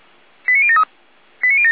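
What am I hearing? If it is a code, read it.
Five-tone selective call (ZVEI), heard as quick runs of five tones, tones of about 70 ms. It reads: 89073
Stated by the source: ZVEI five-tone selective call received with an SDR.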